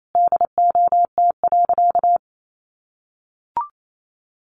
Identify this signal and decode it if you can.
Morse code, heard as a keyed tone, about 28 wpm, 700 Hz. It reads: DOT.
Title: Sentences from Top 1500 Words 28wpm